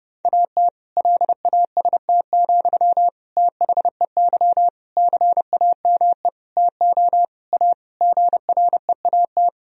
Morse code, heard as a keyed tone, about 30 wpm, 700 Hz. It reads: AT LAST, THEY CAME TO A GREAT
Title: Sentences from Top 1000 Words - Review Part 1 - 30wpm